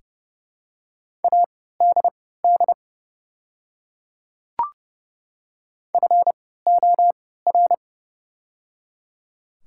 Morse code, heard as a keyed tone, about 31 wpm, spaced as for 17 wpm, 690 Hz. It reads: ADD FOR
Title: 3x Character Spacing - Top 300 Words - 30wpm